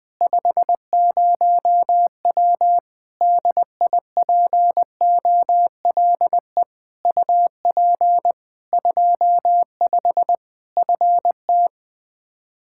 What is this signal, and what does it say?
Morse code, heard as a keyed tone, about 21 wpm, 700 Hz. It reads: 50W DIPOLE UP 25 FT
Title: Practice QSO - Exchange 2 - Info - 20wpm